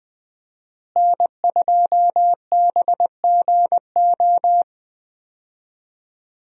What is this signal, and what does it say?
Morse code, heard as a keyed tone, about 20 wpm, 700 Hz. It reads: N2BGO
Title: USA Call Signs - 5 Characters - Repeated 3x - 20wpm